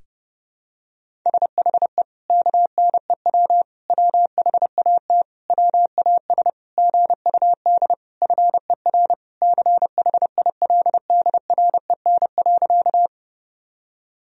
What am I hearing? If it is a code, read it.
Morse code, heard as a keyed tone, about 30 wpm, 700 Hz. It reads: SHE KNEW WHAT WAS GUD FER CHILDREN.
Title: Sentences from Top 400 Words and Abbreviations 30wpm